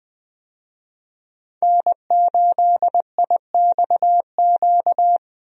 Morse code, heard as a keyed tone, about 20 wpm, 700 Hz. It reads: N8IXQ
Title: USA Call Signs - 5 Characters - Repeated 2x - 20wpm